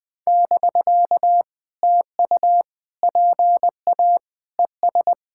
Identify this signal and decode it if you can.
Morse code, heard as a keyed tone, about 20 wpm, 700 Hz. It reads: <BK> TU PA ES
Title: POTA QSO Practice - 20wpm